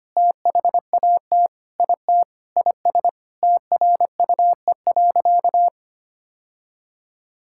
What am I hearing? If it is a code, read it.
Morse code, heard as a keyed tone, about 25 wpm, 700 Hz. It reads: THAT IT IS TRUE.
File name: practice / 25wpm / Sentences with 4-Word Phrases - 25wpm